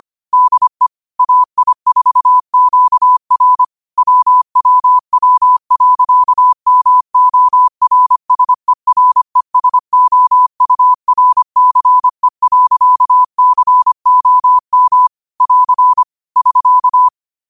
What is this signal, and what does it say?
Morse code, heard as a keyed tone, about 25 wpm, 1000 Hz. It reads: DE AI4QR WWW.MORSERESOURCE.COM <AR> <SK>
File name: qotdfiles / 25wpm